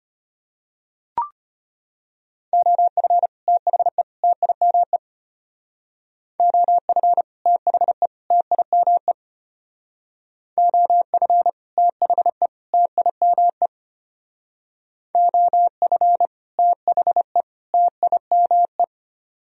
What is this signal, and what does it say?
Morse code, heard as a keyed tone, about 33 wpm, 700 Hz. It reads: OF THE TIME OF THE TIME OF THE TIME OF THE TIME